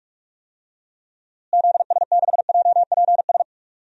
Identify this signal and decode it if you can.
Morse code, heard as a keyed tone, about 46 wpm, 700 Hz. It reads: ZS6JPS
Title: DXCC-100 Call Signs - 6 Characters - Repeated 3x - 45wpm